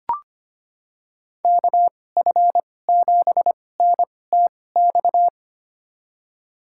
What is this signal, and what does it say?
Morse code, heard as a keyed tone, about 25 wpm, spaced as for 18 wpm, 700 Hz. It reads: KF7NTX
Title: USA Call Signs - 6 Characters - 2x Character Spacing - 25wpm